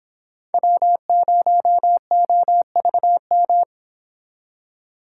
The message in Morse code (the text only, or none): W0OVM